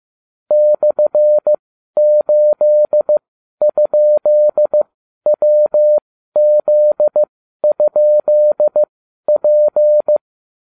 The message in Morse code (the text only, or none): /8?WZ?P